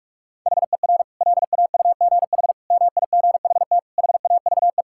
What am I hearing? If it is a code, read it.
Morse code, heard as a keyed tone, about 46 wpm, 700 Hz. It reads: HER LAUGH MIGHT HAVE
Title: Sentences from Top 1000 Words - Review Part 5 - 45wpm